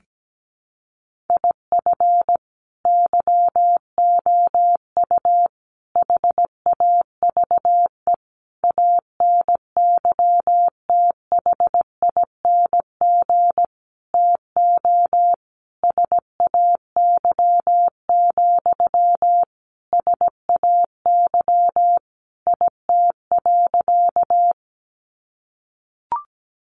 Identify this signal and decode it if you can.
Morse code, heard as a keyed tone, about 17 wpm, 700 Hz. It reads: IF YOU HAVE ANYTHING TO SAY, SAY IT.
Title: Sentences from Top 1000 Words 17wpm